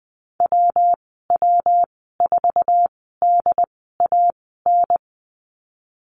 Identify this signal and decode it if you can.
Morse code, heard as a keyed tone, about 20 wpm, spaced as for 15 wpm, 700 Hz. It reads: WW4DAN
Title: 173 Lesson 28g - Number 9 - Call Signs 20wpm